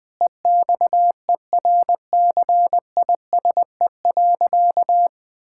Morse code, heard as a keyed tone, about 20 wpm, 700 Hz. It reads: EXERCISE.
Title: Sentences from Top 1000 Words - Review Part 6 - 20wpm